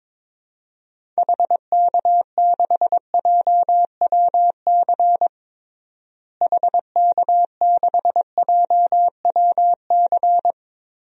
Morse code, heard as a keyed tone, about 21 wpm, 700 Hz. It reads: HK6JWC HK6JWC
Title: DXCC-100 Call Signs - 6 Characters - Repeated 3x - 22wpm